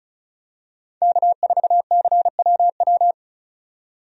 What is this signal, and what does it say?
Morse code, heard as a keyed tone, about 34 wpm, 700 Hz. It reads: K4CWW
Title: USA Call Signs - 5 Characters - 35wpm